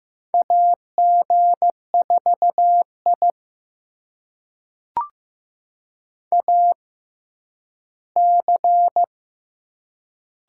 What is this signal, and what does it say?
Morse code, heard as a keyed tone, about 15 wpm, spaced as for 10 wpm, 700 Hz. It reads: AG4I AC
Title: USA Call Signs - 4 Characters - 6x Character Spacing - 15wpm